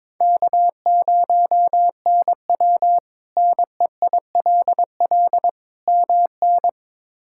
Morse code, heard as a keyed tone, about 21 wpm, 700 Hz. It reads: K0NW NEILL MN
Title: K1USN SST QSO Practice - 22wpm